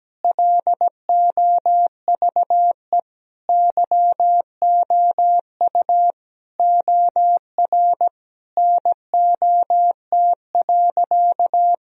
Morse code, heard as a keyed tone, about 17 wpm, 700 Hz. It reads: LOVE YOU OR NOT.
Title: Sentences from Top 1000 Words - Review Part 3 - 17wpm